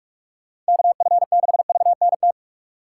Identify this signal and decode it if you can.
Morse code, heard as a keyed tone, about 59 wpm, spaced as for 52 wpm, 700 Hz. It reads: KF6VNT
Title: Speed Racing - USA 2x3 Call Signs 45wpm